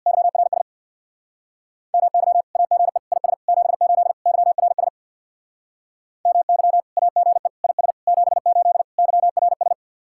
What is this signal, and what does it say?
Morse code, heard as a keyed tone, about 61 wpm, 700 Hz. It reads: YRS MY AGE IS 78 YRS MY AGE IS 78 YRS
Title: Speed-Racing Common QSO Phrases - 50wpm